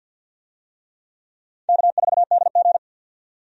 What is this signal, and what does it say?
Morse code, heard as a keyed tone, about 52 wpm, 700 Hz. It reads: K4DG
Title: USA Call Signs - 4 Characters - 50wpm